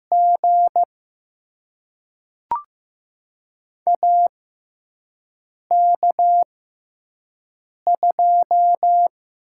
Morse code, heard as a keyed tone, about 15 wpm, spaced as for 5 wpm, 700 Hz. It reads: G AK2
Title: USA Call Signs - 4 Characters - 6x Character Spacing - 15wpm